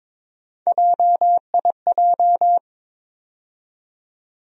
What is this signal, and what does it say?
Morse code, heard as a keyed tone, about 22 wpm, 700 Hz. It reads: JIJ
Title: USA Call Sign Postfixes - 3 Letters 22wpm